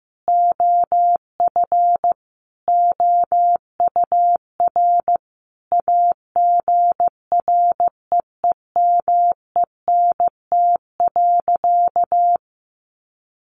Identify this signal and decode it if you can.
Morse code, heard as a keyed tone, about 15 wpm, 700 Hz. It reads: OF OUR AGREEMENT.
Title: Sentences from Top 1500 Words 15wpm